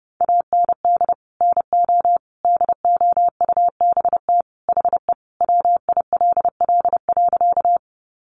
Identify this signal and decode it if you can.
Morse code, heard as a keyed tone, about 30 wpm, 700 Hz. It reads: AND NO DOUBT HE WILL.
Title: Sentences from Top 1500 Words 30wpm